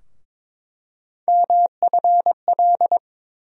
Morse code, heard as a keyed tone, about 21 wpm, 700 Hz. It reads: MFL
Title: USA Call Sign Postfixes - 3 Letters 22wpm